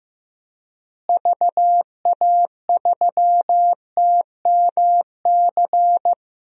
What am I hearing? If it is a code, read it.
Morse code, heard as a keyed tone, about 15 wpm, 700 Hz. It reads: VA3TMC
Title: DXCC-100 Call Signs - 6 Characters - Repeated 2x - 15wpm